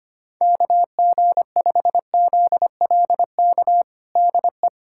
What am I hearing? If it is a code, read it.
Morse code, heard as a keyed tone, about 25 wpm, 700 Hz. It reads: KG5ZLK DE
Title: USA Callsign DE Callsign - 6 Characters - 25wpm